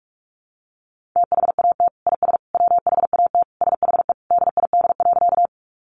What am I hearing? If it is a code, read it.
Morse code, heard as a keyed tone, about 44 wpm, 700 Hz. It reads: THAT IS WHAT SHE DID.
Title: Sentences from Top 100 Words 45wpm